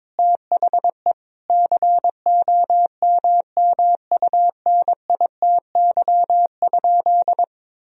Morse code, heard as a keyed tone, about 22 wpm, 700 Hz. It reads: THE COMMUNITY?
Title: Sentences from Top 1000 Words - Review Part 4 - 22wpm